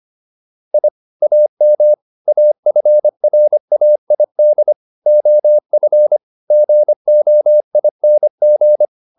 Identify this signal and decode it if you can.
Morse code, heard as a keyed tone, about 25 wpm, 600 Hz. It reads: I AM AFRAID OF GOING